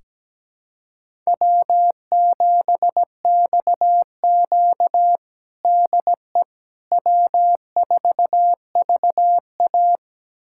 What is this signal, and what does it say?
Morse code, heard as a keyed tone, about 17 wpm, 700 Hz. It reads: W7XQ DE W4VA
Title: USA Callsign DE Callsign - 4 Characters - 17wpm